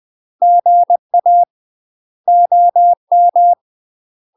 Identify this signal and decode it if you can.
Morse code, heard as a keyed tone, about 20 wpm, 700 Hz. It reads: GA OM
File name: UK MASTER QSOs TRAINING / 20WPM / UK MASTER QSOs 0059